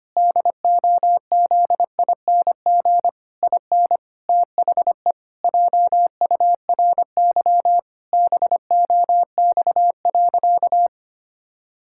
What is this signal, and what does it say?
Morse code, heard as a keyed tone, about 25 wpm, 700 Hz. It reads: DOZING IN THE JURY BOX.
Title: Pangram Challenge v2 - 25wpm